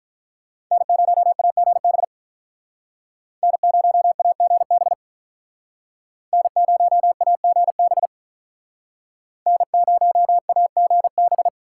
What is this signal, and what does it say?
Morse code, heard as a keyed tone, about 46 wpm, 700 Hz. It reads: N0AGB N0AGB N0AGB N0AGB